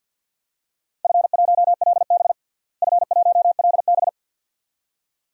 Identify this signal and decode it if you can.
Morse code, heard as a keyed tone, about 52 wpm, 700 Hz. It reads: F1LB F1LB
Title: DXCC-100 Call Signs - 4 Characters - Repeated 2x - 50wpm